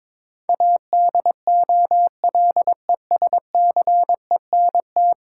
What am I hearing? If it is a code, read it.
Morse code, heard as a keyed tone, about 22 wpm, 700 Hz. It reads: ADOLESCENT
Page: morse-code-ninja-practice-22wpm